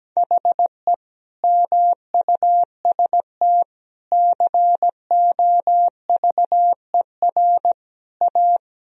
Morse code, heard as a keyed tone, about 17 wpm, 700 Hz. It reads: HE MUST COVER A
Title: Sentences from Top 1000 Words - Review Part 3 - 17wpm